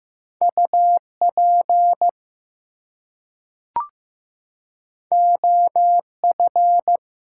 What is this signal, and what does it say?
Morse code, heard as a keyed tone, about 15 wpm, 700 Hz. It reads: UP OF
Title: Two Letter Words - 15wpm